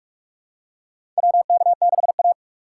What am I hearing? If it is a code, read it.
Morse code, heard as a keyed tone, about 44 wpm, 700 Hz. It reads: WK6A
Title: USA Call Signs - 4x Character Spacing - 45wpm